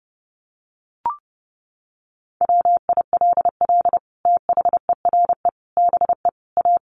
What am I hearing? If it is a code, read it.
Morse code, heard as a keyed tone, about 30 wpm, 700 Hz. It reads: WILL THERE BE A
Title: Sentences from Top 1300 Words 30wpm